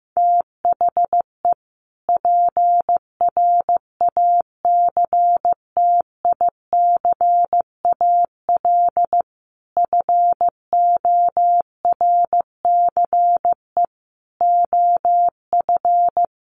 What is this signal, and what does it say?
Morse code, heard as a keyed tone, about 15 wpm, 700 Hz. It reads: THE PRACTICAL FORCE OF
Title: Sentences from Top 1400 Words 15wpm